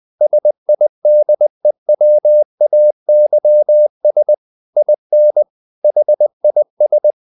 Morse code, heard as a keyed tone, about 20 wpm, 600 Hz. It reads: SIDEWAYS IN HIS
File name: the old man and the sea / the old man and the sea part 1 20wpm